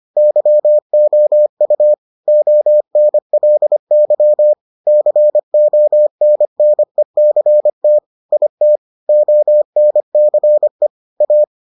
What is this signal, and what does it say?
Morse code, heard as a keyed tone, about 25 wpm, 600 Hz. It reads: YOU ONLY CONNECT IT ONCE A